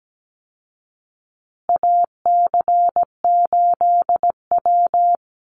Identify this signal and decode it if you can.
Morse code, heard as a keyed tone, about 17 wpm, 700 Hz. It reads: AC8W
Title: Call Signs Repeated 2x - 17wpm